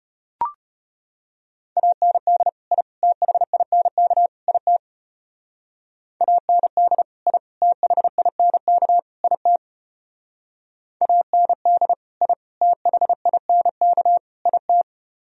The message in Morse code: AND I THINK IT AND I THINK IT AND I THINK IT

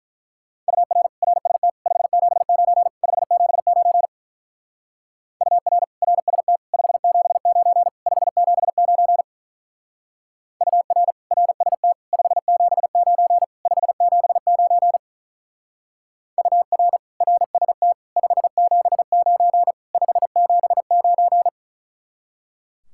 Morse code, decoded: UR RST 579 579 UR RST 579 579 UR RST 579 579 UR RST 579 579

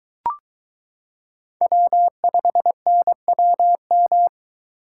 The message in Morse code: W5NWM

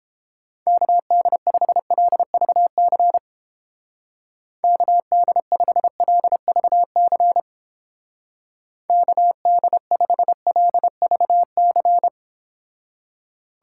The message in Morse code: KD5LVC KD5LVC KD5LVC

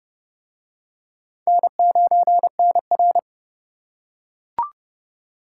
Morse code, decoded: N9NR

